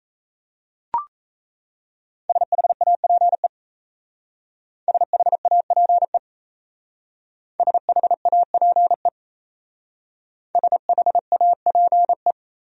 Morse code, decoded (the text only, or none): SHAPE SHAPE SHAPE SHAPE